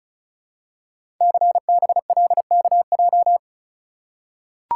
Morse code, heard as a keyed tone, about 36 wpm, 700 Hz. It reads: CBLKJ
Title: Code Groups - 5 Letters - 35wpm